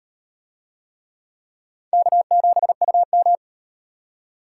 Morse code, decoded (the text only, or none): K7UM